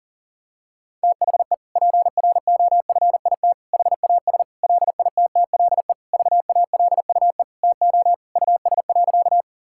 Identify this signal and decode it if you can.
Morse code, heard as a keyed tone, about 40 wpm, 700 Hz. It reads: THE PROFIT HAS LITTLE VALUE TO US.
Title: Sentences from Top 1000 Words - Review Part 5 - 40wpm